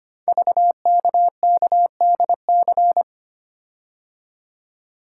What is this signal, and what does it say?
Morse code, heard as a keyed tone, about 25 wpm, 700 Hz. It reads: VKKDC
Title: Code Groups - 5 Letters - 25wpm